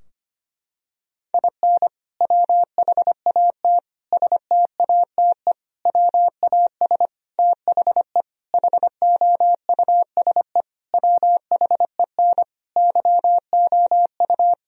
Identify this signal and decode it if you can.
Morse code, heard as a keyed tone, about 25 wpm, 700 Hz. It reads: IN WHAT STATE WAS THE HOUSE WHEN YOU
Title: Sentences from Top 1000 Words - Review Part 2 - 25wpm